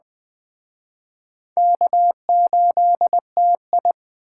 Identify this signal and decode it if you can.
Morse code, heard as a keyed tone, about 21 wpm, 700 Hz. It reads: K8TI
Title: Speed Racing - USA 1x2 Call Signs 17wpm